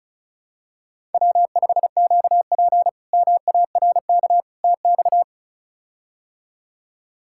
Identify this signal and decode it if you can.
Morse code, heard as a keyed tone, about 34 wpm, 700 Hz. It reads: W5QP MARK TX
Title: K1USN SST QSO Practice - 35wpm